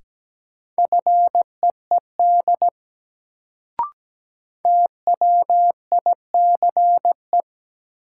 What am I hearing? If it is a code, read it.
Morse code, heard as a keyed tone, about 17 wpm, 700 Hz. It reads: FEED TWICE
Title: Sentences from Top 1600 Words - New Word Drill Alt - 17wpm